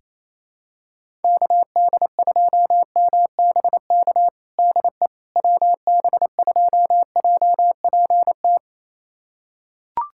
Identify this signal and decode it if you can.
Morse code, heard as a keyed tone, about 28 wpm, 700 Hz. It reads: KD2MBK DE WB2JPT
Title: USA Callsign DE Callsign 28wpm